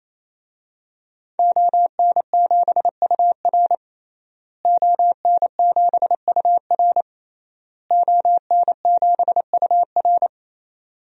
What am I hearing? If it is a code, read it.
Morse code, heard as a keyed tone, about 28 wpm, 700 Hz. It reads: ON7UR ON7UR ON7UR
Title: DXCC-100 Call Signs - 5 Characters - Repeated 3x - 28wpm